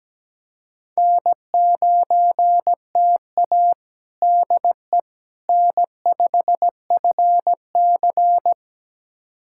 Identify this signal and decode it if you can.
Morse code, heard as a keyed tone, about 17 wpm, 700 Hz. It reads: N9TA DE N5FC
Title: USA Callsign DE Callsign - 4 Characters - 17wpm